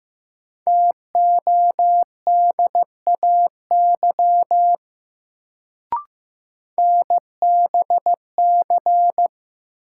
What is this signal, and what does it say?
Morse code, heard as a keyed tone, about 15 wpm, 700 Hz. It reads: TODAY NBC
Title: Sets of 5 Words Encore - New Word Drill - 15wpm